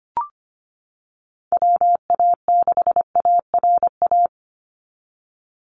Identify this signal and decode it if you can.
Morse code, heard as a keyed tone, about 25 wpm, 700 Hz. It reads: WA6ARA
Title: NA SOTA Chaser Callsigns 25wpm